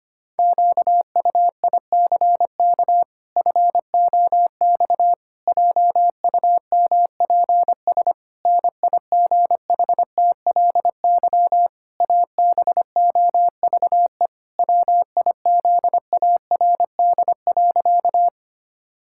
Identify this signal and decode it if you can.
Morse code, heard as a keyed tone, about 25 wpm, 700 Hz. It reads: QUICK FOX JUMPS NIGHTLY ABOVE WIZARD.
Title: Pangram Challenge v2 - 25wpm